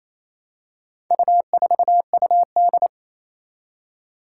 Morse code, heard as a keyed tone, about 28 wpm, 700 Hz. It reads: U4UD
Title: Four Challenging Letter-Number Combinations 28wpm